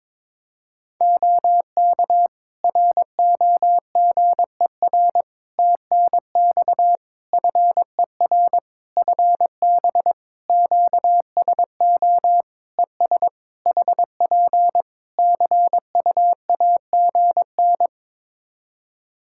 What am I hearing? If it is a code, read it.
Morse code, heard as a keyed tone, about 22 wpm, 700 Hz. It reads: OK ROGER TNX FER FB QSO ES HP CUAGN
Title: Practice QSO - Exchange 4 - Ending - 22wpm